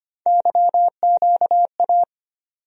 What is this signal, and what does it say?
Morse code, heard as a keyed tone, about 25 wpm, 700 Hz. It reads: YQA
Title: USA Call Sign Postfixes - 3 Letters 25wpm